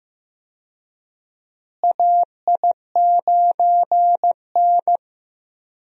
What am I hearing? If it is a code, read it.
Morse code, heard as a keyed tone, about 15 wpm, 700 Hz. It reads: AI9N
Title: USA Call Signs - 4 Characters - Repeated 3x - 15wpm